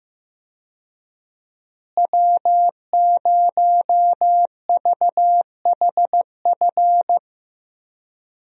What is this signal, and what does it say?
Morse code, heard as a keyed tone, about 15 wpm, 700 Hz. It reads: W0VHF
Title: USA Call Signs - 5 Characters - Repeated 3x - 15wpm